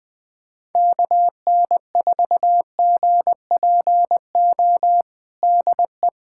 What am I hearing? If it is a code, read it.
Morse code, heard as a keyed tone, about 20 wpm, 700 Hz. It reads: KN4GPO DE
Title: USA Callsign DE Callsign - 6 Characters - 20wpm